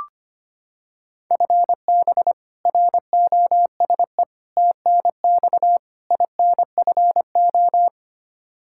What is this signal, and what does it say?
Morse code, heard as a keyed tone, about 25 wpm, 700 Hz. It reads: FB ROSE TNX INFO
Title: Practice QSO - Exchange 3 - More Info - 25wpm